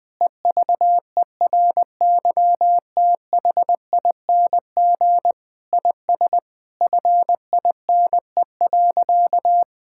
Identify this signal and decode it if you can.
Morse code, heard as a keyed tone, about 20 wpm, 700 Hz. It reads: EVERYTHING IS FINE.
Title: Sentences with 6-Word Phrases - Spoken Beginning - 20wpm